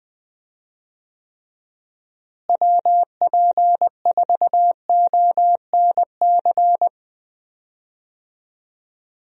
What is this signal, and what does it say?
Morse code, heard as a keyed tone, about 20 wpm, 700 Hz. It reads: WP4ONC
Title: USA Call Signs - 6 Characters - Repeated 3x - 20wpm